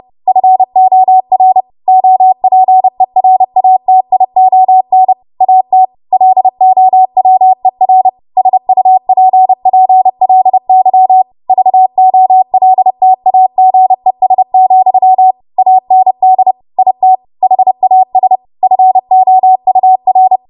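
Morse code, heard as a keyed tone, about 30 wpm, 750 Hz. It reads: FOR OPERATION AT LOWER SUPPLY VOLTAGES, AND IT HAS FOUR
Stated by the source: W1AW Web Code Practice Files, 30 WPM run from ARRL; text from April 2023 QST.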